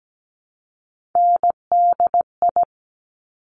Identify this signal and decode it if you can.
Morse code, heard as a keyed tone, about 17 wpm, 700 Hz. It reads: NDI
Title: Top Three Letter Combinations in Words 17wpm